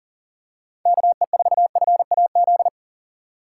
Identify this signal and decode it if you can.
Morse code, heard as a keyed tone, about 39 wpm, 700 Hz. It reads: KE4FAZ
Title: USA Call Signs - 6 Characters - 3x Character Spacing - 40wpm